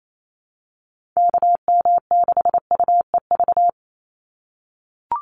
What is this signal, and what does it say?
Morse code, heard as a keyed tone, about 28 wpm, 700 Hz. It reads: KM6UEV
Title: USA Call Signs 28wpm